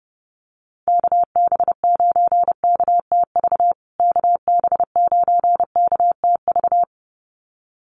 Morse code, heard as a keyed tone, about 30 wpm, 700 Hz. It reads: KB9KTV KB9KTV
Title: USA Call Signs Repeated 2x 30wpm